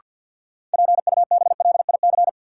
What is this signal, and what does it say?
Morse code, heard as a keyed tone, about 47 wpm, 700 Hz. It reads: PUBLIC
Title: Sentences from Top 600 Words - New Word Drill - 50wpm